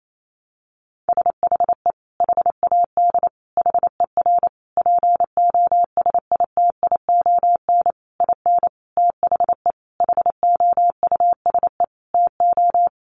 Morse code, heard as a keyed tone, about 28 wpm, 700 Hz. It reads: SHE HAD HER POSITION IN THE HOUSE TO
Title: Sentences from Top 900 Words 28wpm